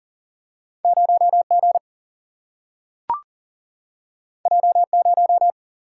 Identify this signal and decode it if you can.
Morse code, heard as a keyed tone, about 40 wpm, 700 Hz. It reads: 0 G J 0
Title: Single-Letter Single-Number 40wpm